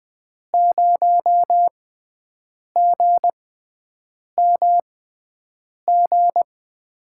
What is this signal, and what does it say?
Morse code, heard as a keyed tone, about 20 wpm, spaced as for 7 wpm, 700 Hz. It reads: 0GMG
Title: USA Call Signs - 6 Characters - 6x Character Spacing - 20wpm